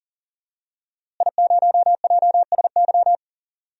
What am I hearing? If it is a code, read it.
Morse code, heard as a keyed tone, about 39 wpm, 700 Hz. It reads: I0JSY
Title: DXCC-100 Call Signs - 5 Characters - Repeated 3x - 40wpm